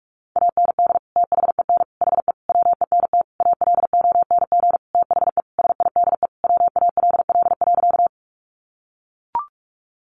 Morse code, heard as a keyed tone, about 44 wpm, 700 Hz. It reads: AND THEN HE WENT ALONG THE SIDE WALL.
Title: Sentences from Top 1100 Words 45wpm